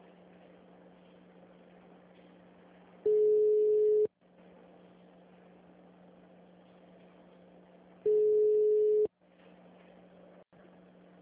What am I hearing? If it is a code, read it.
Ringback tone (Europe, 425 Hz).